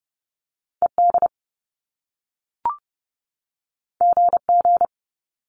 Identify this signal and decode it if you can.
Morse code, heard as a keyed tone, about 30 wpm, 700 Hz. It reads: E D G G
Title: Top Two Letter Combinations in Words 30wpm